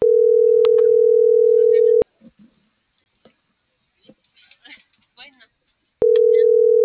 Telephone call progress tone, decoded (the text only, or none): ringback tone